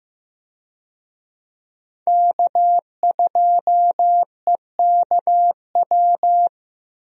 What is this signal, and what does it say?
Morse code, heard as a keyed tone, about 15 wpm, 700 Hz. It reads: K2EKW
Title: USA Call Signs - 5 Characters - Repeated 3x - 15wpm